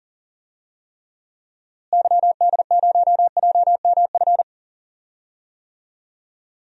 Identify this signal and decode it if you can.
Morse code, heard as a keyed tone, about 37 wpm, 700 Hz. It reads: YD0JMF
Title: DXCC-100 Call Signs - 6 Characters - Repeated 3x - 40wpm